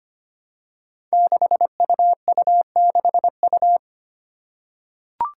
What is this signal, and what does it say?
Morse code, heard as a keyed tone, about 25 wpm, 700 Hz. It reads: 6UU6U
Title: Five Challenging Letter-Number Combinations 25wpm